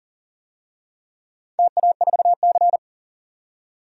Morse code, heard as a keyed tone, about 40 wpm, 700 Hz. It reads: TA4C
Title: DXCC-100 Call Signs - 4 Characters - 40wpm